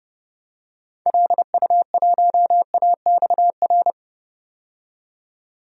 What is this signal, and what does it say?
Morse code, heard as a keyed tone, about 31 wpm, 700 Hz. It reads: LU1AXR
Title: DXCC-100 Call Signs - 6 Characters - 30wpm